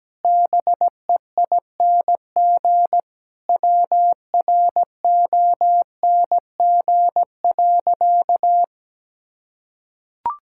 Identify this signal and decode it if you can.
Morse code, heard as a keyed tone, about 17 wpm, 700 Hz. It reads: BEING WRONG.
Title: Sentences from Top 1000 Words - Review Part 6 - 17wpm